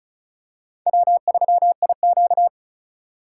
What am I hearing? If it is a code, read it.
Morse code, heard as a keyed tone, about 36 wpm, 700 Hz. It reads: W3IQ
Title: USA Call Signs - 4 Characters - 3x Character Spacing - 35wpm